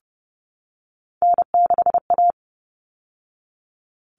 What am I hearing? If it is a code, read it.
Morse code, heard as a keyed tone, about 30 wpm, 700 Hz. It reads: N6A
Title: All Cut Numbers - 3-Digit 30wpm